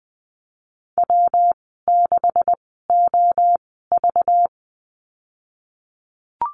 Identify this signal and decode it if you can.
Morse code, heard as a keyed tone, about 20 wpm, spaced as for 15 wpm, 700 Hz. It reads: W6OV
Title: 193 Lesson 31g - Letter v - Call Signs 20wpm